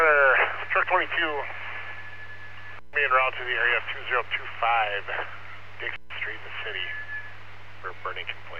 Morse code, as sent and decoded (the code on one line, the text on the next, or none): none